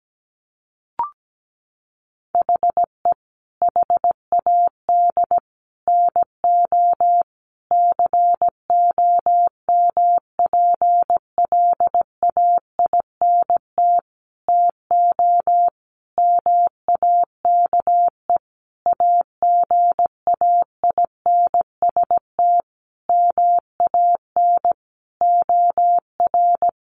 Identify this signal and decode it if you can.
Morse code, heard as a keyed tone, about 17 wpm, 700 Hz. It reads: HE HAD NO COMPLAINT TO MAKE AGAINST MAN OR
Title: Sentences from Top 1400 Words 17wpm